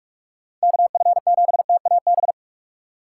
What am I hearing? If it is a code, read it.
Morse code, heard as a keyed tone, about 46 wpm, 700 Hz. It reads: KF7TAB
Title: USA Call Signs - 6 Characters - 2x Character Spacing - 45wpm